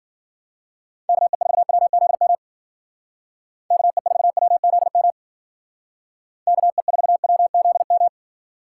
Morse code, heard as a keyed tone, about 52 wpm, 700 Hz. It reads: KE4WZM KE4WZM KE4WZM